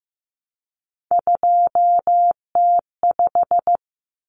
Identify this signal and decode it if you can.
Morse code, heard as a keyed tone, about 15 wpm, 700 Hz. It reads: 2T5
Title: Common Cut Numbers - 3-Digit 15wpm